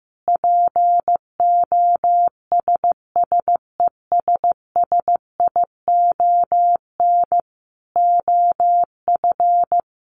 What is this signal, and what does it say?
Morse code, heard as a keyed tone, about 15 wpm, 700 Hz. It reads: POSSESSION OF
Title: Sentences from Top 1400 Words 15wpm